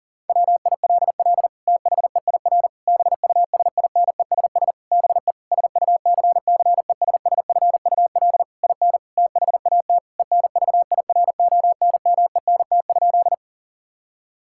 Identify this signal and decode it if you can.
Morse code, heard as a keyed tone, about 40 wpm, 700 Hz. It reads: WILL THEIR BUSINESS BE SUCCESSFUL IN THAT ENVIRONMENT?
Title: Sentences from Top 1000 Words - Review Part 5 - 40wpm